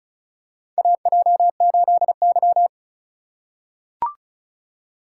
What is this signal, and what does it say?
Morse code, heard as a keyed tone, about 36 wpm, 700 Hz. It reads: AJ8Y
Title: USA Call Signs - 4 Characters - 2x Character Spacing - 35wpm